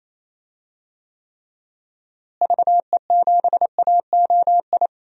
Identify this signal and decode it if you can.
Morse code, heard as a keyed tone, about 28 wpm, 700 Hz. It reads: VE7AOI
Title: DXCC-100 Call Signs - 6 Characters - Repeated 3x - 28wpm